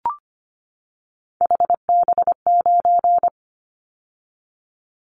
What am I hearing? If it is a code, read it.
Morse code, heard as a keyed tone, about 25 wpm, 700 Hz. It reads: HB9
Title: Callsign Prefixes for Top 25 DXCC - 25wpm